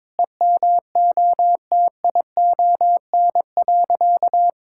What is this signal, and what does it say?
Morse code, heard as a keyed tone, about 22 wpm, 700 Hz. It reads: EMOTION.
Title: Sentences from Top 1600 Words 22wpm